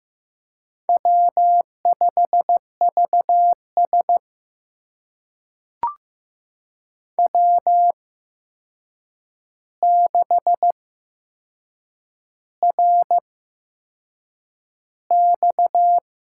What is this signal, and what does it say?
Morse code, heard as a keyed tone, about 15 wpm, spaced as for 6 wpm, 700 Hz. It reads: W5VS W6RX
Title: USA Call Signs - 4 Characters - 8x Character Spacing - 15wpm